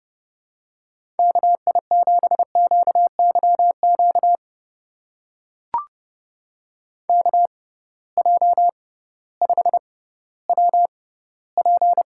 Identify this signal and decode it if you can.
Morse code, heard as a keyed tone, about 32 wpm, spaced as for 17 wpm, 700 Hz. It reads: KI7QYQ KJ5WP
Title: USA Call Signs - 6x Character Spacing - 30wpm